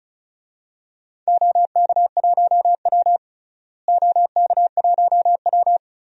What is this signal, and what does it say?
Morse code, heard as a keyed tone, about 36 wpm, 700 Hz. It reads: OK1W OK1W
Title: DXCC-100 Call Signs - 4 Characters - Repeated 2x - 35wpm